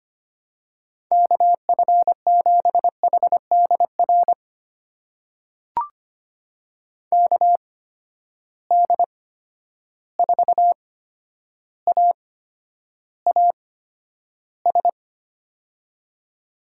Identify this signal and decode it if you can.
Morse code, heard as a keyed tone, about 25 wpm, spaced as for 11 wpm, 700 Hz. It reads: KF7HDR KD4AAS
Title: USA Call Signs - 6 Characters - 8x Character Spacing - 25wpm